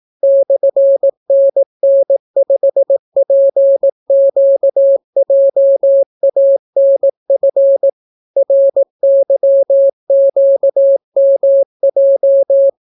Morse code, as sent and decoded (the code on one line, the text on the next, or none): -..-. -. -. ..... .--. --.- .--- .- -. ..-. / .-. -.-- --.- -- .---
/NN5PQJANF RYQMJ